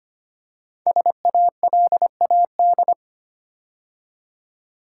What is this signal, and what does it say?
Morse code, heard as a keyed tone, about 25 wpm, 700 Hz. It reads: SALAD
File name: 5 Letter Words - First 3 Letters Given - 25wpm